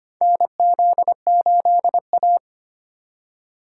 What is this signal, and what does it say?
Morse code, heard as a keyed tone, about 25 wpm, 700 Hz. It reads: NZ8A
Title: USA Call Signs - 4 Characters - 4x Character Spacing - 25wpm